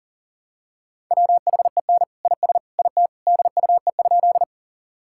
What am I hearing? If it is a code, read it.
Morse code, heard as a keyed tone, about 40 wpm, 700 Hz. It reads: WHEN IS IT DUE?
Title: Sentences from Top 1000 Words - Review Part 6 - 40wpm